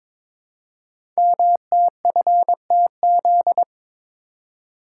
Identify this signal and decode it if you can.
Morse code, heard as a keyed tone, about 22 wpm, 700 Hz. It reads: MTFTZ
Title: Code Groups - 5 Letter:Numbers - 22wpm